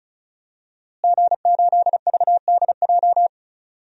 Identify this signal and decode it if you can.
Morse code, heard as a keyed tone, about 34 wpm, 700 Hz. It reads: G8VDJ
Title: DXCC-100 Call Signs - 5 Characters - Repeated 2x - 35wpm